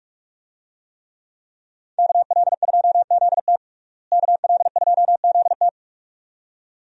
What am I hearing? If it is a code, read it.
Morse code, heard as a keyed tone, about 47 wpm, 700 Hz. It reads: KL2ZT KL2ZT
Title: USA Call Signs - 5 Characters - Repeated 2x - 45wpm